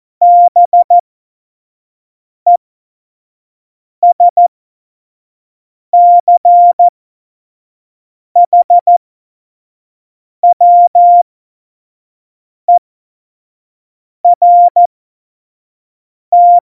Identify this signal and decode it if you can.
Morse code, heard as a keyed tone, about 14 wpm, spaced as for 5 wpm, 700 Hz. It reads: BESCHWERT
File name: Morsekurs / Witze / Witz 21 Juli 2020, 25 F70